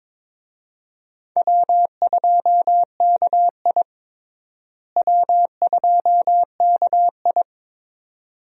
Morse code, heard as a keyed tone, about 22 wpm, 700 Hz. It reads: W2KI W2KI
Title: USA Call Signs - 4 Characters - Repeated 2x - 22wpm